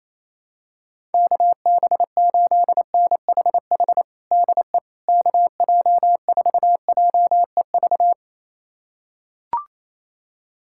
Morse code, heard as a keyed tone, about 28 wpm, 700 Hz. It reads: KB8NHH DE KJ4JEV
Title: USA Callsign DE Callsign 28wpm